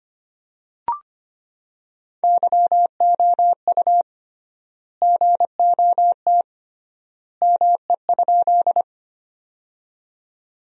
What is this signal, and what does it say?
Morse code, heard as a keyed tone, about 25 wpm, 700 Hz. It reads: YOU GOT ME?
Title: 3x Word-Spacing - Sentences from Top 300 Words - 25wpm